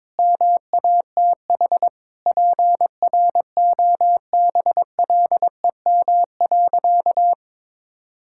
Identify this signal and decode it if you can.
Morse code, heard as a keyed tone, about 22 wpm, 700 Hz. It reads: MATH PROBLEM.
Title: Sentences from Top 1000 Words - Review Part 6 - 22wpm